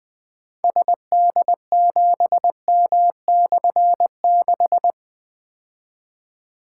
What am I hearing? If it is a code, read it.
Morse code, heard as a keyed tone, about 20 wpm, 700 Hz. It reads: SD7M/6
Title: Difficult Call Signs 20wpm